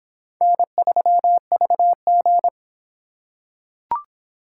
Morse code, heard as a keyed tone, about 26 wpm, 700 Hz. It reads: N3VG